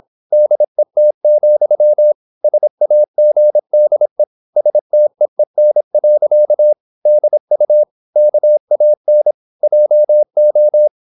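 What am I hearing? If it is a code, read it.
Morse code, heard as a keyed tone, about 26 wpm, 600 Hz. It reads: DET, SAGDE STEEN. DU KAN JO
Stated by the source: High-speed (QRQ) Morse at 26 wpm.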